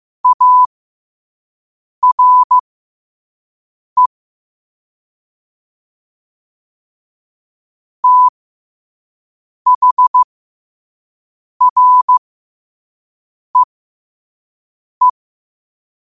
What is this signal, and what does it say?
Morse code, heard as a keyed tone, about 15 wpm, spaced as for 5 wpm, 1000 Hz. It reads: ARE THREE
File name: qotdfiles / 5wpm